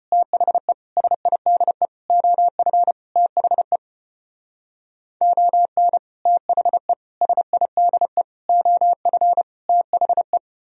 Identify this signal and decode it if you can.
Morse code, heard as a keyed tone, about 33 wpm, 700 Hz. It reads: THE SIDE OF THE ON THE SIDE OF THE